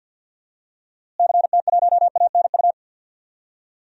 Morse code, heard as a keyed tone, about 52 wpm, 700 Hz. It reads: CT1ANU
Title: DXCC-100 Call Signs - 6 Characters - Repeated 3x - 50wpm